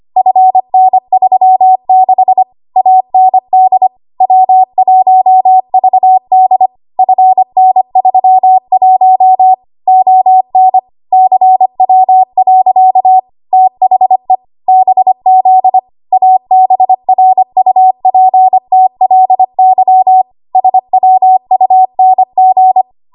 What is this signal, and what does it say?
Morse code, heard as a keyed tone, about 25 wpm, 750 Hz. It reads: FN36 AND W1VD FN31 ON CW. THE BZ ABRUPTLY SWUNG
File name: Morse / 25WPM